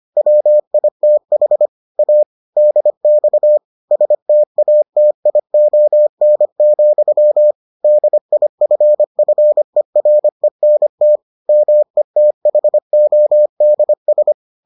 Morse code, heard as a keyed tone, about 25 wpm, 600 Hz. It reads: WITH A DX STATION, DIFFERENT METHODS